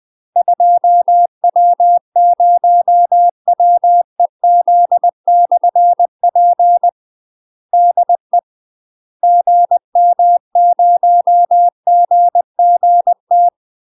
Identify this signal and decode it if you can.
Morse code, heard as a keyed tone, about 20 wpm, 700 Hz. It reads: 2W0WEZ/P DE GM0GGT
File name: UK MASTER QSOs TRAINING / 20WPM / UK MASTER QSOs 0071